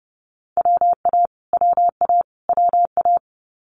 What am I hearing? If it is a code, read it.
Morse code, heard as a keyed tone, about 30 wpm, 700 Hz. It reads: WA WA WA
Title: US States Repeated 3x 30wpm